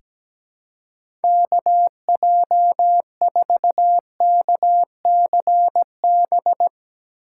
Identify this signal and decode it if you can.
Morse code, heard as a keyed tone, about 17 wpm, 700 Hz. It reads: KJ4KCB
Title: USA Call Signs - 6 Characters - 4x Character Spacing - 17wpm